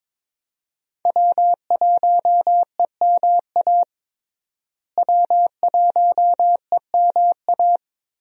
Morse code, heard as a keyed tone, about 22 wpm, 700 Hz. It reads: W1EMA W1EMA
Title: USA Call Signs - 5 Characters - Repeated 3x - 22wpm